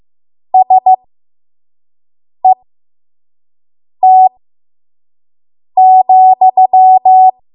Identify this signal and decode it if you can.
Morse code, heard as a keyed tone, about 15 wpm, spaced as for 5 wpm, 750 Hz. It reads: SET,
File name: Morse / 05WPM